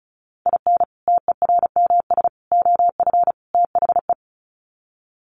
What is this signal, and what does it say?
Morse code, heard as a keyed tone, about 34 wpm, 700 Hz. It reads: IN TERMS OF THE
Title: Sets of 4 Words 35wpm